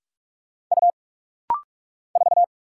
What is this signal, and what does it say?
Morse code, heard as a keyed tone, about 44 wpm, 690 Hz. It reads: U 4
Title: Single Letter-Number - Rapid-Fire 45wpm